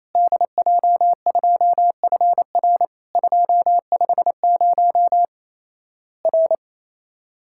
Morse code, heard as a keyed tone, about 28 wpm, 700 Hz. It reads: DJ2FR 250 R
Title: Worked All Europe DX Contest - QTC Exchange - 28wpm